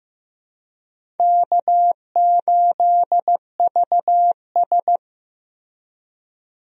Morse code, heard as a keyed tone, about 15 wpm, 700 Hz. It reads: K8VS
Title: USA Call Signs - 4 Characters - Repeated 3x - 15wpm